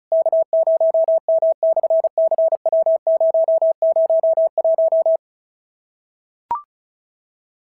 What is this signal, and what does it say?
Morse code, heard as a keyed tone, about 36 wpm, 650 Hz. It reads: K0M/CW001
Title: SOTA QSO Practice 35wpm